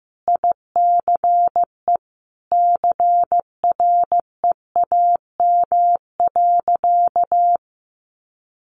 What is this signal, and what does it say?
Morse code, heard as a keyed tone, about 15 wpm, 700 Hz. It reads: ICE CREAM.
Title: Sentences from Top 1500 Words 15wpm